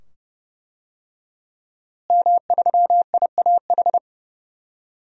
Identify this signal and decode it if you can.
Morse code, heard as a keyed tone, about 30 wpm, 700 Hz. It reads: M3IAH
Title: DXCC-100 Call Signs - 5 Characters - Repeated 2x - 30wpm